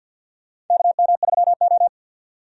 Morse code, heard as a keyed tone, about 52 wpm, 700 Hz. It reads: KM3O
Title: USA Call Signs - 4 Characters - 50wpm